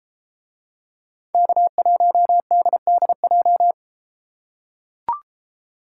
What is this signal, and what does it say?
Morse code, heard as a keyed tone, about 31 wpm, 700 Hz. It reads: K1DDJ